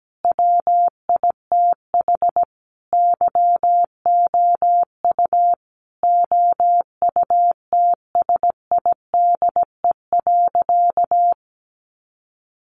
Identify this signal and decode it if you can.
Morse code, heard as a keyed tone, about 17 wpm, 700 Hz. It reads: WITH YOU OUTSIDE.
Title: Sentences from Top 900 Words 17wpm